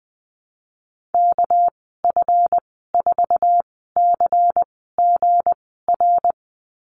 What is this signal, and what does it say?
Morse code, heard as a keyed tone, about 20 wpm, spaced as for 15 wpm, 700 Hz. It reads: KF4CGR
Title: 238 Lesson 38g - Letter q - Call Signs 20wpm